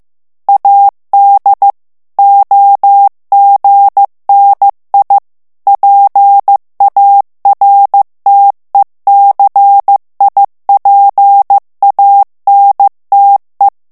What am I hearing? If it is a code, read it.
Morse code, heard as a keyed tone, about 15 wpm, 800 Hz. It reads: AD OGNI PARTECIPANTE
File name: morse mp3 / ITALIEN texte clair 2 vitesse 14 mots minute